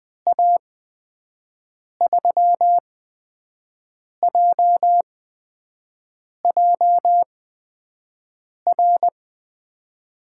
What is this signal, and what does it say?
Morse code, heard as a keyed tone, about 20 wpm, spaced as for 5 wpm, 700 Hz. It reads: A3JJR
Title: USA Call Signs - 6 Characters - 8x Character Spacing - 20wpm